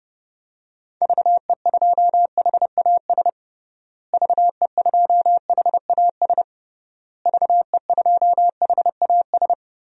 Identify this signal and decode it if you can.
Morse code, heard as a keyed tone, about 30 wpm, 700 Hz. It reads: VE2HAS VE2HAS VE2HAS
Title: DXCC-100 Call Signs - 6 Characters - Repeated 3x - 30wpm